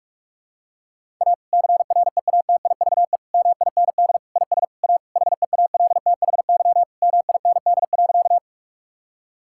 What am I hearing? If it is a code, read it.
Morse code, heard as a keyed tone, about 46 wpm, 700 Hz. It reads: A CREATIVE MIND IS A HEALTHY MIND.
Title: Sentences from Top 1000 Words - Review Part 5 - 45wpm